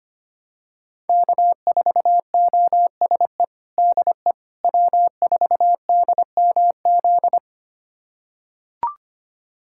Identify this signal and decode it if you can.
Morse code, heard as a keyed tone, about 25 wpm, 700 Hz. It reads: K4OSE DE W4DMZ
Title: USA Callsign DE Callsign - 5 Characters - 25wpm